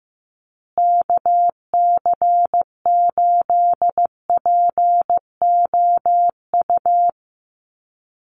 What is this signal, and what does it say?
Morse code, heard as a keyed tone, about 15 wpm, 700 Hz. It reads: KC8POU
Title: USA Call Signs Repeated 3x 15wpm